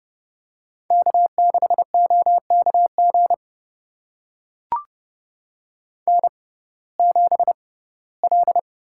Morse code, heard as a keyed tone, about 32 wpm, spaced as for 19 wpm, 700 Hz. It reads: K6OKG N7L
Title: USA Call Signs - 5 Characters - 6x Character Spacing - 30wpm